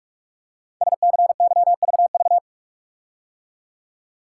Morse code, heard as a keyed tone, about 44 wpm, 690 Hz. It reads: SCYVV
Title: Code Groups - 5 Letters - 45wpm